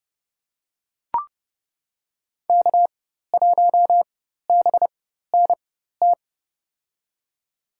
Morse code, heard as a keyed tone, about 30 wpm, spaced as for 14 wpm, 700 Hz. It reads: K1BNT
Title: USA Call Signs - 5 Characters - 4x Character Spacing - 30wpm